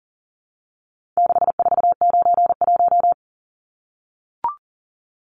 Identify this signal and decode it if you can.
Morse code, heard as a keyed tone, about 40 wpm, 700 Hz. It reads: 6491
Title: Numbers - 4-Digit 40wpm